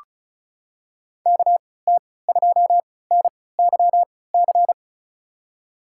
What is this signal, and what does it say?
Morse code, heard as a keyed tone, about 36 wpm, spaced as for 20 wpm, 700 Hz. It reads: KT2NYC
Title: USA Call Signs - 3x Character Spacing - 35wpm